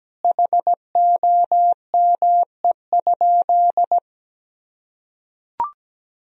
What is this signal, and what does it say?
Morse code, heard as a keyed tone, about 17 wpm, 700 Hz. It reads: HOME?
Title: Sentences from Top 1000 Words - Review Part 2 - 17wpm